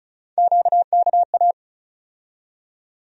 Morse code, heard as a keyed tone, about 36 wpm, 700 Hz. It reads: QKA
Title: USA Call Sign Postfixes - 3 Letters 35wpm